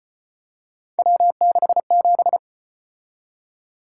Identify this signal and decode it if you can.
Morse code, heard as a keyed tone, about 34 wpm, 700 Hz. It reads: W67